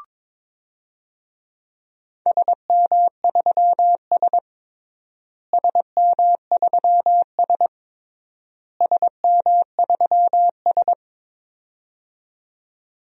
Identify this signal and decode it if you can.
Morse code, heard as a keyed tone, about 22 wpm, 700 Hz. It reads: SM3S SM3S SM3S
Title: DXCC-100 Call Signs - 4 Characters - Repeated 3x - 22wpm